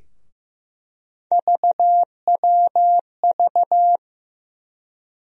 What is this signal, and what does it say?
Morse code, heard as a keyed tone, about 15 wpm, 700 Hz. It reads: VWV
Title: USA Call Sign Postfixes - 3 Letters 15wpm